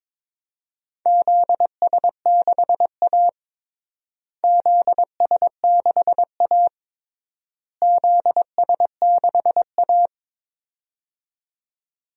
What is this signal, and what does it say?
Morse code, heard as a keyed tone, about 22 wpm, 700 Hz. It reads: ZS6A ZS6A ZS6A
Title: DXCC-100 Call Signs - 4 Characters - Repeated 3x - 22wpm